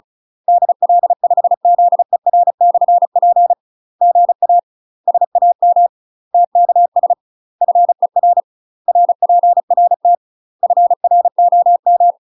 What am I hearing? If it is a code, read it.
Morse code, heard as a keyed tone, about 37 wpm, 700 Hz. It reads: DL5ZER/P GA SAM TKS FER RPRT FROM
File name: UK MASTER QSOs TRAINING / 35WPM / UK MASTER QSOs 0003